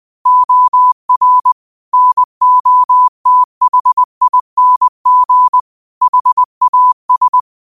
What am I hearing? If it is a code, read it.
Morse code, heard as a keyed tone, about 20 wpm, 1000 Hz. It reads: OR NOTHING HAS